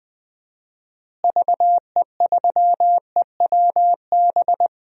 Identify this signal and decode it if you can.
Morse code, heard as a keyed tone, about 20 wpm, 700 Hz. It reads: VE3EWB
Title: DXCC-100 Call Signs - 6 Characters - Repeated 2x - 20wpm